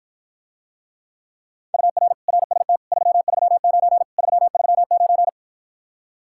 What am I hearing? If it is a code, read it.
Morse code, heard as a keyed tone, about 52 wpm, 700 Hz. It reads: UR RST 339 339